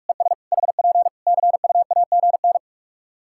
Morse code, heard as a keyed tone, about 41 wpm, 700 Hz. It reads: ES HP CUAGN
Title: K1USN SST QSO Practice 45wpm